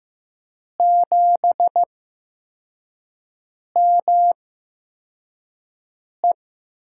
Morse code, heard as a keyed tone, about 15 wpm, spaced as for 4 wpm, 700 Hz. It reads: 7ME
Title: USA Call Signs - 5 Characters - 8x Character Spacing - 15wpm